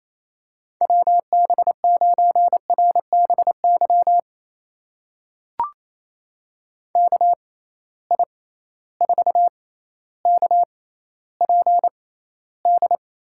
Morse code, heard as a keyed tone, about 28 wpm, spaced as for 14 wpm, 700 Hz. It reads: WB9RBY KI4KPD
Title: USA Call Signs - 6 Characters - 6x Character Spacing - 28wpm